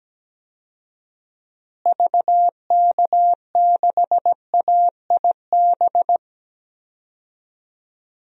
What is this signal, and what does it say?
Morse code, heard as a keyed tone, about 17 wpm, 700 Hz. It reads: VK6AIB
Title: DXCC-100 Call Signs - 6 Characters - Repeated 2x - 17wpm